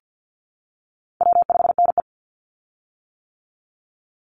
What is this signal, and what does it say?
Morse code, heard as a keyed tone, about 50 wpm, 700 Hz. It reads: W5NE
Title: Call Signs - 50wpm